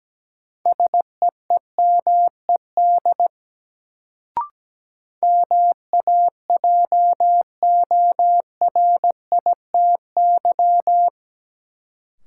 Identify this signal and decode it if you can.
Morse code, heard as a keyed tone, about 17 wpm, 700 Hz. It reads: SEEMED MAJORITY
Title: Sets of 3 Words - New Word Drill - 17wpm